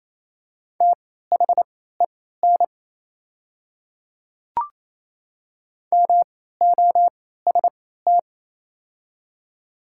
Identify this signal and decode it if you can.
Morse code, heard as a keyed tone, about 27 wpm, spaced as for 16 wpm, 700 Hz. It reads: THEN MOST
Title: 3x Character Spacing - Top 200 Words - 28wpm